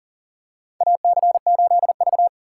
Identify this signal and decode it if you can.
Morse code, heard as a keyed tone, about 39 wpm, 690 Hz. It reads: AC8V
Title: USA Call Signs - 2x Character Spacing - 40wpm